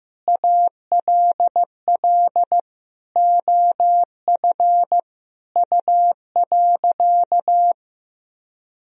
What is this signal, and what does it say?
Morse code, heard as a keyed tone, about 15 wpm, 700 Hz. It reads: ALL OF U.
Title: Sentences from Top 500 Words and Abbreviations 15wpm